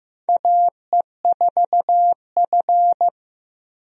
Morse code, heard as a keyed tone, about 15 wpm, 700 Hz. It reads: AE4F
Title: USA Call Signs - 4 Characters - Repeated 3x - 15wpm